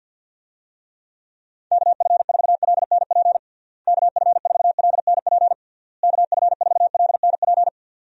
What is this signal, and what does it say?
Morse code, heard as a keyed tone, about 49 wpm, 700 Hz. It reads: KF4LNP KF4LNP KF4LNP
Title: USA Call Signs - 6 Characters - Repeated 3x - 50wpm